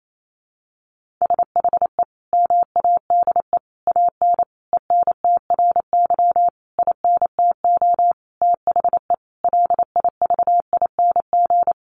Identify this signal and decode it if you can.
Morse code, heard as a keyed tone, about 28 wpm, 700 Hz. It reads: SHE MADE AN ENTRY INTO THE LIVING
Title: Sentences from Top 1500 Words 28wpm